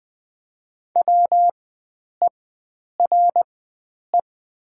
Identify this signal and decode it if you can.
Morse code, heard as a keyed tone, about 20 wpm, spaced as for 9 wpm, 700 Hz. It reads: WERE
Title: 4x Character Spacing - Top 300 Words - 20wpm